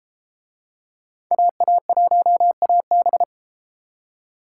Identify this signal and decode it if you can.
Morse code, heard as a keyed tone, about 31 wpm, 700 Hz. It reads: AA1AB